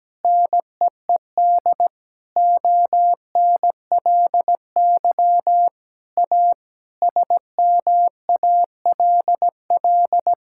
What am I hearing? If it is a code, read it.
Morse code, heard as a keyed tone, about 17 wpm, 700 Hz. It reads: NEED ONLY A SMALL
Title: Sentences from Top 1000 Words - Review Part 6 - 17wpm